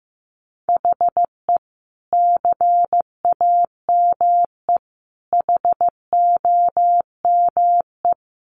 Morse code, heard as a keyed tone, about 15 wpm, 700 Hz. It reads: HE CAME HOME
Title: Sentences from Top 1400 Words 15wpm